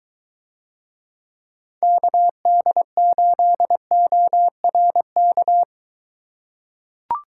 Morse code, heard as a keyed tone, about 22 wpm, 700 Hz. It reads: KD8ORK